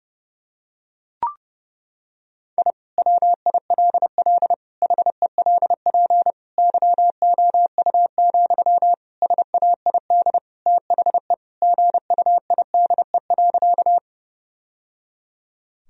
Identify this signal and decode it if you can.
Morse code, heard as a keyed tone, about 32 wpm, 700 Hz. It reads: I WILL HELP YOU, SAID THE GUIDE.
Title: Sentences from Top 1000 Words - Review Part 3 - 30wpm